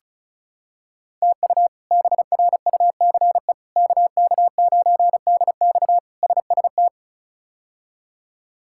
Morse code, heard as a keyed tone, about 36 wpm, 700 Hz. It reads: TU BRUCE KK9DX SST
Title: K1USN SST QSO Practice - 35wpm